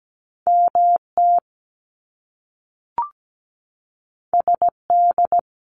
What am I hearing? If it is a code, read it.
Morse code, heard as a keyed tone, about 17 wpm, 700 Hz. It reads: MT SD
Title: US States Repeated 3x 17wpm